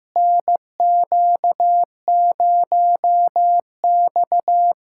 Morse code, heard as a keyed tone, about 15 wpm, 700 Hz. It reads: NQ0X
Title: USA Callsign DE Callsign 15wpm